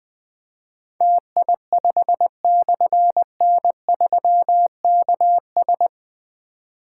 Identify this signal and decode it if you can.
Morse code, heard as a keyed tone, about 20 wpm, 700 Hz. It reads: TI5/N3KS
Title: Difficult Call Signs 20wpm